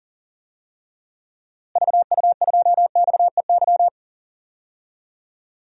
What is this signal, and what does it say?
Morse code, heard as a keyed tone, about 40 wpm, 700 Hz. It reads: VU2XEY
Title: DXCC-100 Call Signs - 6 Characters - Repeated 2x - 40wpm